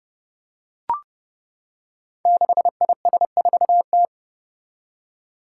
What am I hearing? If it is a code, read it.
Morse code, heard as a keyed tone, about 30 wpm, 700 Hz. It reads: 6IS4T
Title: Five Challenging Letter-Number Combinations 30wpm